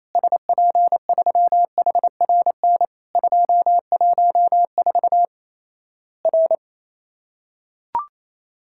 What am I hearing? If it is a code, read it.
Morse code, heard as a keyed tone, about 27 wpm, 700 Hz. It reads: SP3HRN 214 R
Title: Worked All Europe DX Contest - QTC Exchange - 28wpm